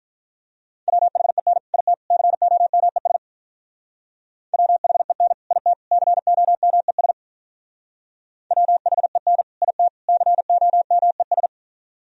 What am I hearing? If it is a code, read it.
Morse code, heard as a keyed tone, about 46 wpm, 700 Hz. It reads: WHEN IT COMES WHEN IT COMES WHEN IT COMES